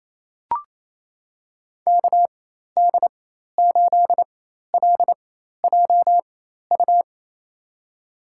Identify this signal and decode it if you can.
Morse code, heard as a keyed tone, about 28 wpm, spaced as for 13 wpm, 700 Hz. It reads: KD8LJU
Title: USA Call Signs - 6 Characters - 4x Character Spacing - 28wpm